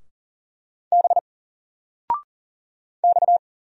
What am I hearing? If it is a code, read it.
Morse code, heard as a keyed tone, about 40 wpm, 700 Hz. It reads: B X
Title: Single Letter-Number 40wpm